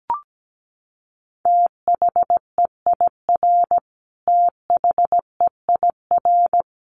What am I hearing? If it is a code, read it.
Morse code, heard as a keyed tone, about 17 wpm, 700 Hz. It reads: THEIR THEIR
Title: Top 300 Words Repeated 17wpm